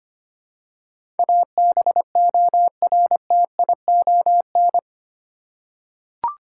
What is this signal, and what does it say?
Morse code, heard as a keyed tone, about 25 wpm, 700 Hz. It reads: ABORTION
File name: 8 Letter Words - First 3 Letters Given - 25wpm